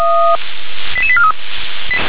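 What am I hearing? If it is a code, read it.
Five-tone selective call (ZVEI), heard as quick runs of five tones, tones of about 80 ms. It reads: 88843
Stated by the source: ZVEI five-tone alert received by SDR.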